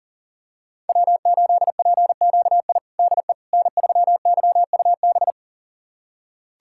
Morse code, heard as a keyed tone, about 40 wpm, 700 Hz. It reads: W8PQI DE N3YUB
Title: USA Callsign DE Callsign - 5 Characters - 40wpm